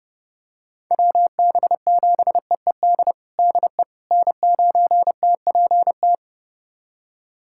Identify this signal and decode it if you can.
Morse code, heard as a keyed tone, about 30 wpm, 700 Hz. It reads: WB7EED DE N9TPT
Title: USA Callsign DE Callsign 30wpm